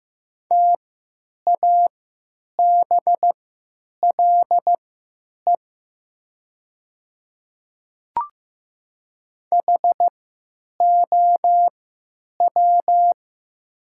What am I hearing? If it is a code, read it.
Morse code, heard as a keyed tone, about 15 wpm, spaced as for 9 wpm, 700 Hz. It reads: TABLE HOW
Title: 3x Character Spacing - Top 500 Words - 15wpm